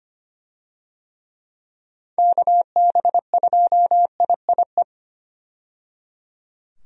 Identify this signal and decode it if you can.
Morse code, heard as a keyed tone, about 25 wpm, 700 Hz. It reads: KB2IIE
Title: USA Call Signs - 6 Characters - 25wpm